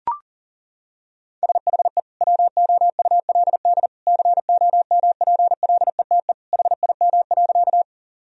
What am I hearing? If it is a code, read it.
Morse code, heard as a keyed tone, about 37 wpm, 700 Hz. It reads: SHE WOULD COMPLETE HIM.
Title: Sentences from Top 1000 Words - Review Part 2 - 40wpm